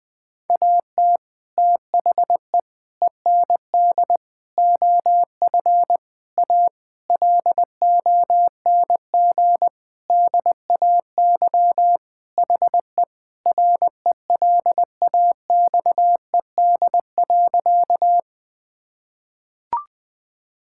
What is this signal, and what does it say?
Morse code, heard as a keyed tone, about 20 wpm, 700 Hz. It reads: AT THE END OF A LONG DAY HE RELAXED.
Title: Sentences with 5-Word Phrases - 20wpm